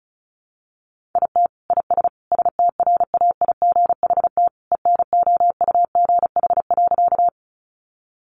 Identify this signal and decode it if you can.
Morse code, heard as a keyed tone, about 34 wpm, 700 Hz. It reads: IT IS STRAIGHT ENOUGH.
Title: Sentences from Top 1000 Words 35wpm